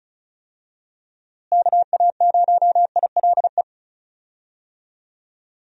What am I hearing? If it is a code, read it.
Morse code, heard as a keyed tone, about 36 wpm, 700 Hz. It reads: KA0ILE
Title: USA Call Signs - 6 Characters - Repeated 2x - 35wpm